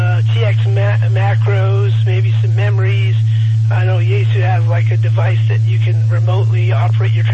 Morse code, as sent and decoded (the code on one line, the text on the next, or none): none